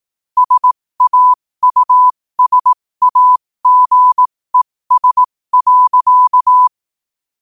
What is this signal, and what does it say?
Morse code, heard as a keyed tone, about 18 wpm, spaced as for 15 wpm, 1000 Hz. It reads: SAUSAGES.
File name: qotdfiles / 15wpm